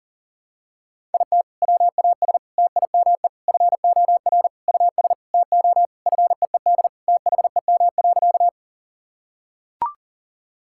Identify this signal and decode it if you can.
Morse code, heard as a keyed tone, about 40 wpm, 700 Hz. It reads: IT WAS TIME FOR US TO FEED THEM.
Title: Sentences from Top 1600 Words 40wpm